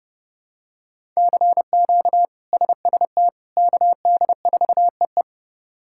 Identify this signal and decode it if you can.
Morse code, heard as a keyed tone, about 32 wpm, 700 Hz. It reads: CQ SST KD4EE
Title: K1USN SST QSO Practice - 30wpm